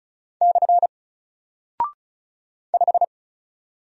Morse code, heard as a keyed tone, about 36 wpm, 700 Hz. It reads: / 5
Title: Single Character (Letter:Number:Punctuation:Prosign) 35wpm